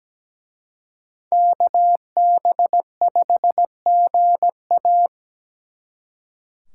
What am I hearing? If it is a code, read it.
Morse code, heard as a keyed tone, about 20 wpm, 700 Hz. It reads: KB5GA
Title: Speed Racing - USA Call Signs 17wpm